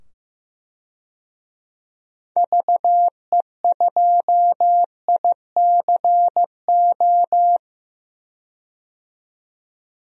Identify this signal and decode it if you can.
Morse code, heard as a keyed tone, about 15 wpm, 700 Hz. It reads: VE2ICO
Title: DXCC-100 Call Signs - 6 Characters - 15wpm